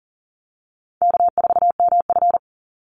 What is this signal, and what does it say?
Morse code, heard as a keyed tone, about 42 wpm, 700 Hz. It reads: K4MF
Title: NA SOTA Chaser Callsigns 40wpm